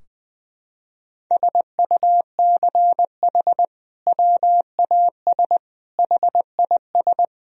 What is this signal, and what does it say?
Morse code, heard as a keyed tone, about 20 wpm, 700 Hz. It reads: SUCH WAS HIS
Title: Sentences from Top 1000 Words - Review Part 3 - 20wpm